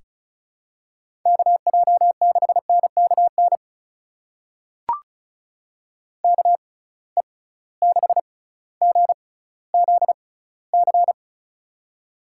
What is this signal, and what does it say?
Morse code, heard as a keyed tone, about 34 wpm, spaced as for 18 wpm, 700 Hz. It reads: KJ6NKN KE6GZC
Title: USA Call Signs - 6 Characters - 6x Character Spacing - 35wpm